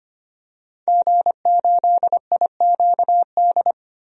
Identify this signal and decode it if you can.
Morse code, heard as a keyed tone, about 25 wpm, 700 Hz. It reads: G8IQD
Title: DXCC-100 Call Signs - 5 Characters - Repeated 2x - 25wpm